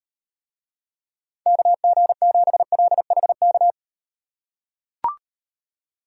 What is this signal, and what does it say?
Morse code, heard as a keyed tone, about 38 wpm, 700 Hz. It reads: KG7LHK